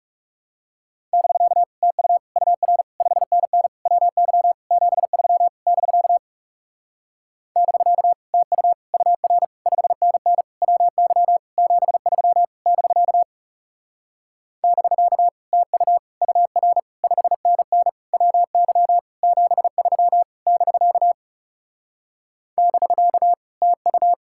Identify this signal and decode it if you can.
Morse code, heard as a keyed tone, about 36 wpm, 700 Hz. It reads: <BK> TU UR 5NN WY 73 <BK> <BK> TU UR 5NN WY 73 <BK> <BK> TU UR 5NN WY 73 <BK> <BK> TU
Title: Speed-Racing Common QSO Phrases Encore - 30wpm